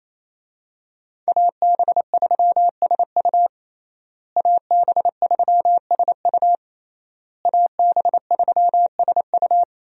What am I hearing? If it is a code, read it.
Morse code, heard as a keyed tone, about 28 wpm, 700 Hz. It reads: AB3SU AB3SU AB3SU
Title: USA Call Signs - 5 Characters - Repeated 3x - 28wpm